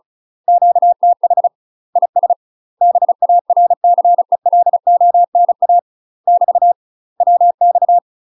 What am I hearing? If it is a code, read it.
Morse code, heard as a keyed tone, about 36 wpm, 700 Hz. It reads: QTH IS BARCELONA <BT> WX
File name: UK MASTER QSOs TRAINING / 35WPM / UK MASTER QSOs 0026